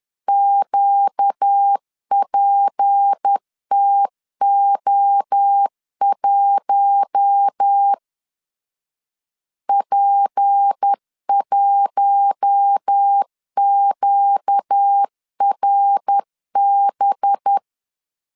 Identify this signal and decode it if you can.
Morse code, heard as a keyed tone, about 10 wpm, 800 Hz. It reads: QPTO1 P1QRB
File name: LEZIONI / 6 PKX0 / 6 P 10wpm 2432